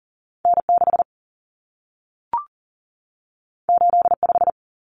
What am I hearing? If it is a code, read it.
Morse code, heard as a keyed tone, about 40 wpm, 700 Hz. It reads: N 6 8 5
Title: Common Cut Numbers - 2-Digit 40wpm